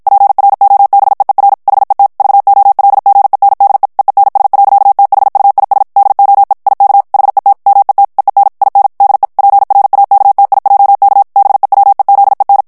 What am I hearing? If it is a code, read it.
Morse code, heard as a keyed tone, about 55 wpm, 800 Hz. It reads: PROBEER HET VOLGENDE EENS. THUIS DOE IK HET MET EEN IN DE PRAKTIJK BEWEZEN
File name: HOLLANDAIS texte clair 2 vitesse 50 mots minute